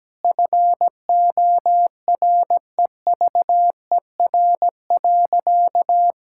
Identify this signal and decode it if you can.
Morse code, heard as a keyed tone, about 17 wpm, 700 Hz. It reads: FOREVER.
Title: Sentences from Top 1600 Words 17wpm